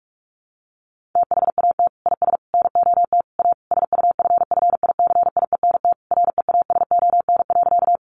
Morse code, heard as a keyed tone, about 44 wpm, 700 Hz. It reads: THAT IS NOT A SUFFICIENT REASON.
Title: Sentences from Top 1400 Words 45wpm